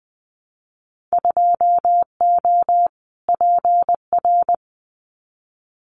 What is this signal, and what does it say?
Morse code, heard as a keyed tone, about 20 wpm, 700 Hz. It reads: 2O PR
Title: Winter Field Day Exchanges Encore 20wpm